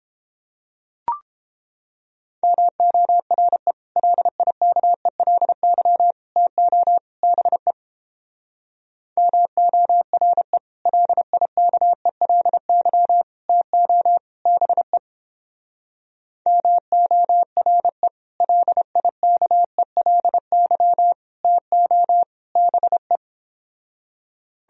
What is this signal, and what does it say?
Morse code, heard as a keyed tone, about 29 wpm, 700 Hz. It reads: MORE LIKELY TO BE MORE LIKELY TO BE MORE LIKELY TO BE